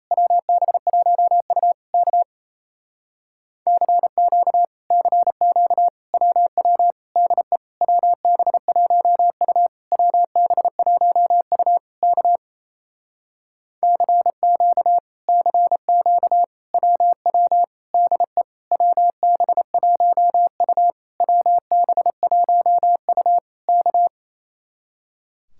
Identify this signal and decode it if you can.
Morse code, heard as a keyed tone, about 33 wpm, 700 Hz. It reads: WB1U K CQ CQ WW DE WB1U WB1U K CQ CQ WW DE WB1U WB1U K
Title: Speed-Racing Common QSO Phrases - 28wpm